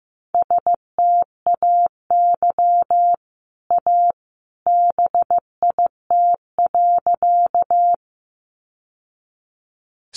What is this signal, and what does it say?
Morse code, heard as a keyed tone, about 15 wpm, 700 Hz. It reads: STAY A BIT.
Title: Sentences from Top 900 Words 15wpm